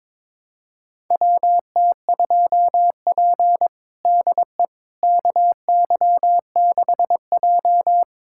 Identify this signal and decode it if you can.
Morse code, heard as a keyed tone, about 22 wpm, 700 Hz. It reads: WT2P DE KY6J
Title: USA Callsign DE Callsign - 4 Characters - 22wpm